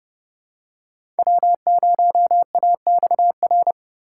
Morse code, heard as a keyed tone, about 30 wpm, 700 Hz. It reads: W0AXR
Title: USA Call Signs - 5 Characters - 4x Character Spacing - 30wpm